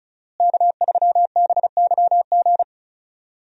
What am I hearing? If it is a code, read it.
Morse code, heard as a keyed tone, about 34 wpm, 700 Hz. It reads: K3BYG
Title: USA Call Signs - 5 Characters - Repeated 3x - 35wpm